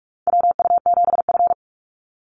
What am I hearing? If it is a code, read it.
Morse code, heard as a keyed tone, about 44 wpm, 700 Hz. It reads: WU7F
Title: Call Signs - 45wpm